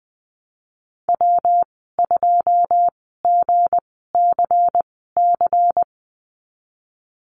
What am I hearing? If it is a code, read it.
Morse code, heard as a keyed tone, about 20 wpm, spaced as for 14 wpm, 700 Hz. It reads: W2GCC
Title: 207 Lesson 33g - Letter j - Call Signs 20wpm